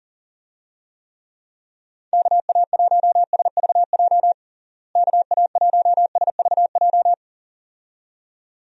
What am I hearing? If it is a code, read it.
Morse code, heard as a keyed tone, about 40 wpm, 700 Hz. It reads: KA1SVJ KA1SVJ
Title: USA Call Signs - 6 Characters - Repeated 2x - 40wpm